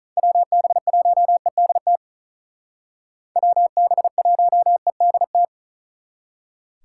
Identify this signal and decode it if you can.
Morse code, heard as a keyed tone, about 46 wpm, 700 Hz. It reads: WB1EDT WB1EDT